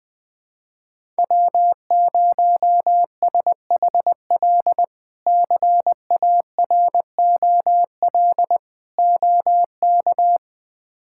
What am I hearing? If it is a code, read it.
Morse code, heard as a keyed tone, about 21 wpm, 700 Hz. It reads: W0SHL CAROL OK
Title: K1USN SST QSO Practice - 20wpm